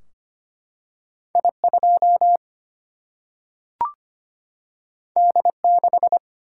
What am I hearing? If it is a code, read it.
Morse code, heard as a keyed tone, about 25 wpm, 700 Hz. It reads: I 2 D 6
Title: Single-Letter Single-Number 25wpm